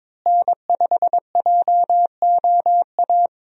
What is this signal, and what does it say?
Morse code, heard as a keyed tone, about 22 wpm, 700 Hz. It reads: N5JOA
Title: USA Call Signs - 5 Characters - Repeated 3x - 22wpm